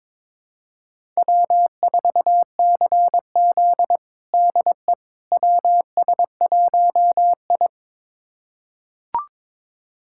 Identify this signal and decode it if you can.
Morse code, heard as a keyed tone, about 22 wpm, 700 Hz. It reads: W4CZ DE WS1I
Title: USA Callsign DE Callsign - 4 Characters - 22wpm